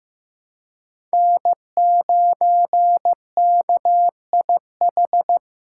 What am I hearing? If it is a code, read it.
Morse code, heard as a keyed tone, about 15 wpm, 700 Hz. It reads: N9KIH
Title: USA Call Signs - 5 Characters - 3x Character Spacing - 15wpm